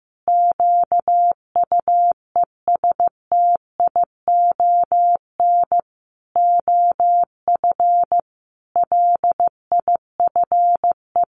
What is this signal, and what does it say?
Morse code, heard as a keyed tone, about 15 wpm, 700 Hz. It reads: QUESTION OF LIFE
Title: Sentences from Top 1100 Words 15wpm